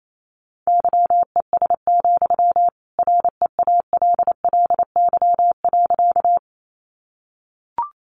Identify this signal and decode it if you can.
Morse code, heard as a keyed tone, about 28 wpm, 700 Hz. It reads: YES, REALLY.
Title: Sentences from Top 600 Words 28wpm